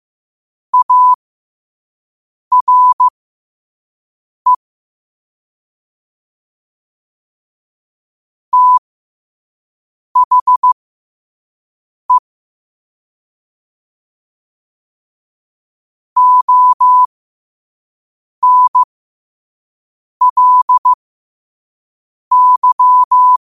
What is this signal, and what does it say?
Morse code, heard as a keyed tone, about 15 wpm, spaced as for 5 wpm, 1000 Hz. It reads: ARE THE ONLY
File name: qotdfiles / 5wpm